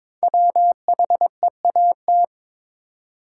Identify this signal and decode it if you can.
Morse code, heard as a keyed tone, about 22 wpm, 700 Hz. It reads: WHEAT
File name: 5 Letter Words - First 3 Letters Given - 22wpm